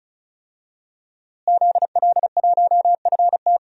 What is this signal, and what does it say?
Morse code, heard as a keyed tone, about 34 wpm, 700 Hz. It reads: ZL1FT
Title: DXCC-100 Call Signs - 5 Characters - Repeated 2x - 35wpm